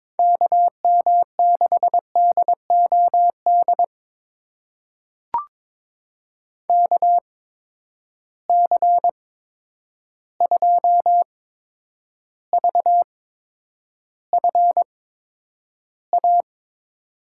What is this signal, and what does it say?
Morse code, heard as a keyed tone, about 23 wpm, spaced as for 10 wpm, 700 Hz. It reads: KM6DOD KC2VFA
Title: USA Call Signs - 6 Characters - 8x Character Spacing - 22wpm